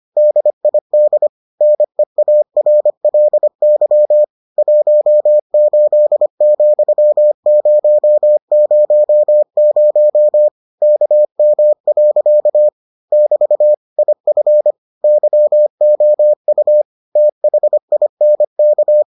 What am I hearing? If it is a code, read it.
Morse code, heard as a keyed tone, about 25 wpm, 600 Hz. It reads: DID NEARLY 18,000 KM. <BT> IF YOU THINK